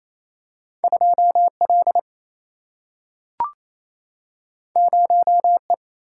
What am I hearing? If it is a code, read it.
Morse code, heard as a keyed tone, about 28 wpm, 700 Hz. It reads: 2L 0E
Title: Single-Letter Single-Number 28wpm